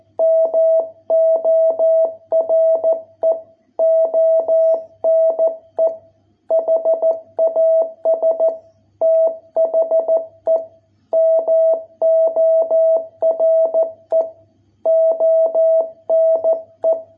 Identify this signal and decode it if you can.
Morse code, heard as a keyed tone, about 14 wpm, 660 Hz. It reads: MORE ONE HAS THE MORE ONE